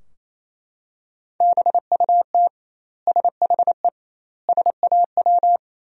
Morse code, heard as a keyed tone, about 28 wpm, 700 Hz. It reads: BUT SHE SAW
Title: 2x Word-Spacing - Sentences from Top 300 Words - 28wpm